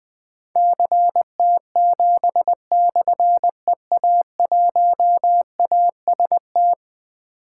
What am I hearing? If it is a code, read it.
Morse code, heard as a keyed tone, about 20 wpm, 700 Hz. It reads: CT7/EA1AST
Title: Difficult Call Signs 20wpm